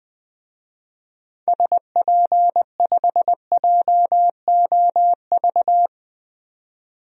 Morse code, heard as a keyed tone, about 20 wpm, 700 Hz. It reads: SP5JOV
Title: DXCC-100 Call Signs - 6 Characters - 20wpm